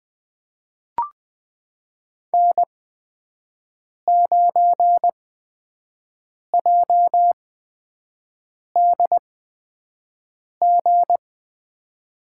Morse code, heard as a keyed tone, about 20 wpm, spaced as for 5 wpm, 700 Hz. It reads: N9JDG
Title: USA Call Signs - 5 Characters - 8x Character Spacing - 20wpm